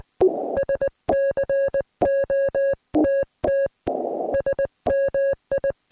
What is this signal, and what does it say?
Morse code, heard as a keyed tone, about 19 wpm, 560 Hz. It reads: SCOTT SMI